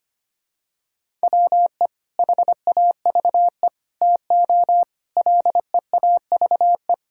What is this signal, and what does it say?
Morse code, heard as a keyed tone, about 25 wpm, 700 Hz. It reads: WE HAVE TO LEAVE
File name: practice / 25wpm / Sentences with 3-Word Phrases - 25wpm